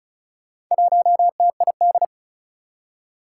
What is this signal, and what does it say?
Morse code, heard as a keyed tone, about 36 wpm, 700 Hz. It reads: 1TID
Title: Four Challenging Letter-Number Combinations 35wpm